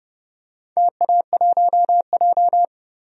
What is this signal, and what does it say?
Morse code, heard as a keyed tone, about 30 wpm, 700 Hz. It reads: TA1J
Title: DXCC-100 Call Signs - 4 Characters - Repeated 3x - 30wpm